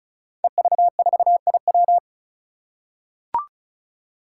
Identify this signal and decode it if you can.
Morse code, heard as a keyed tone, about 36 wpm, 700 Hz. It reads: EV4IW
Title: Five Challenging Letter-Number Combinations 35wpm